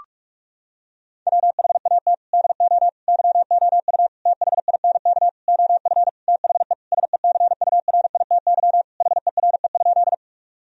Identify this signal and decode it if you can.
Morse code, heard as a keyed tone, about 46 wpm, 700 Hz. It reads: WHAT DO YOU THINK OF THE SECURITY HERE?
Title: Sentences from Top 1000 Words - Review Part 4 - 45wpm